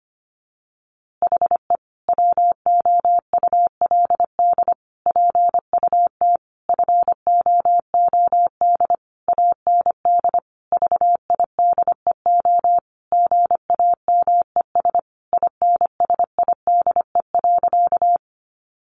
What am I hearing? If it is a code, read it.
Morse code, heard as a keyed tone, about 25 wpm, 700 Hz. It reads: HE WOULD PUT FOOD AND VIDEO GAMES INSIDE.
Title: Easy Phrases 25wpm